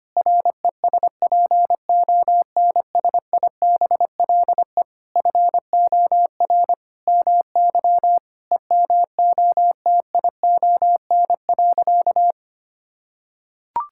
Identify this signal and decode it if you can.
Morse code, heard as a keyed tone, about 25 wpm, 700 Hz. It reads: RESPONSIBLE FOR MY EMOTION.
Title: Sentences from Top 1600 Words 25wpm